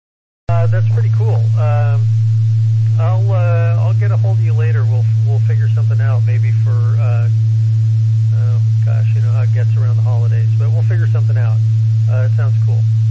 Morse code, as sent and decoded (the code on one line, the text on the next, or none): none